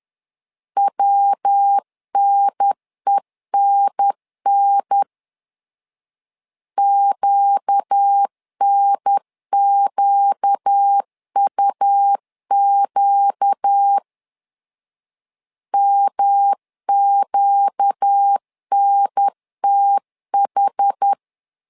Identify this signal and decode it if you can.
Morse code, heard as a keyed tone, about 10 wpm, 800 Hz. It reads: WNENN QNQUQ MQNTH
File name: LEZIONI / 4 Q 10wpm 2210